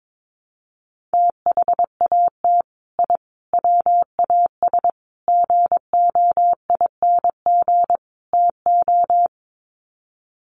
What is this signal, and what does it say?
Morse code, heard as a keyed tone, about 22 wpm, 700 Hz. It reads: THAT I WAS GOING TO
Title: Sets of 5 Words Encore 22wpm